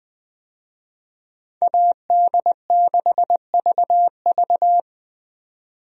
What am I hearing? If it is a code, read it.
Morse code, heard as a keyed tone, about 20 wpm, 700 Hz. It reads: AD6VV
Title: USA Call Signs - 5 Characters - 20wpm